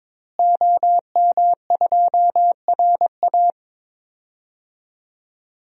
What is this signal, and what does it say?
Morse code, heard as a keyed tone, about 21 wpm, 700 Hz. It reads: OM2RA
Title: DXCC-100 Call Signs - 5 Characters - Repeated 2x - 22wpm